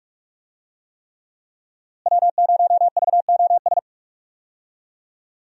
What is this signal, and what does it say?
Morse code, heard as a keyed tone, about 45 wpm, 700 Hz. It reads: W0VOS
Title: USA Call Signs - 5 Characters - Repeated 3x - 45wpm